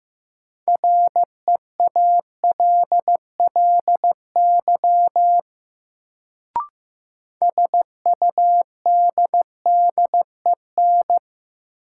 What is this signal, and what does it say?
Morse code, heard as a keyed tone, about 15 wpm, 700 Hz. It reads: REALLY SUDDEN
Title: Sets of 4 Words - New Word Drill - 15wpm